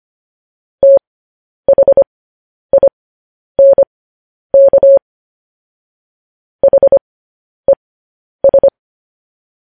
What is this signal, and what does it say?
Morse code, heard as a keyed tone, about 25 wpm, spaced as for 10 wpm, 550 Hz. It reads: THINK HES
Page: news headlines in morse code at 10 wpm